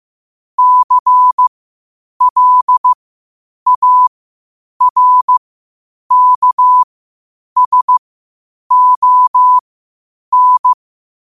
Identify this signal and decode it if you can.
Morse code, heard as a keyed tone, about 15 wpm, spaced as for 8 wpm, 1000 Hz. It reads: CLARKSON
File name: qotdfiles / 8wpm